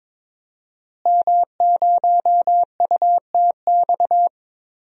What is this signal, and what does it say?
Morse code, heard as a keyed tone, about 22 wpm, 700 Hz. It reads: M0UTX
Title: DXCC-100 Call Signs - 5 Characters - Repeated 2x - 22wpm